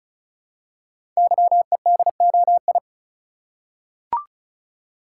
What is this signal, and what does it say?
Morse code, heard as a keyed tone, about 36 wpm, 700 Hz. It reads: YEDOI
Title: Code Groups - 5 Letters - 35wpm